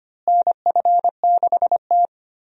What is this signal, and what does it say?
Morse code, heard as a keyed tone, about 25 wpm, 700 Hz. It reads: NF6T
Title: USA Call Signs - 4 Characters - Repeated 2x - 25wpm